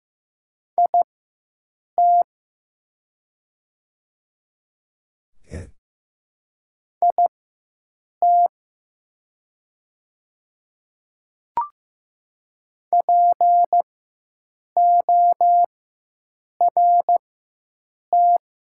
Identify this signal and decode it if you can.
Morse code, heard as a keyed tone, about 15 wpm, spaced as for 7 wpm, 700 Hz. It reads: IT IT PORT
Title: 4x Character Spacing - Top 500 Words - 15wpm